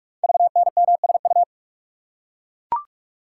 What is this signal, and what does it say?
Morse code, heard as a keyed tone, about 44 wpm, 690 Hz. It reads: VNMSU
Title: Code Groups - 5 Letters - 45wpm